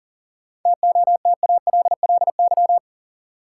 Morse code, heard as a keyed tone, about 40 wpm, 700 Hz. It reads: TOTALLY
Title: Sentences from Top 1600 Words - New Word Drill Alt - 40wpm